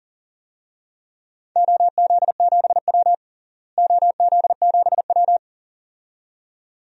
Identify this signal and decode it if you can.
Morse code, heard as a keyed tone, about 40 wpm, 700 Hz. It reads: OZ7W OZ7W
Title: DXCC-100 Call Signs - 4 Characters - Repeated 2x - 40wpm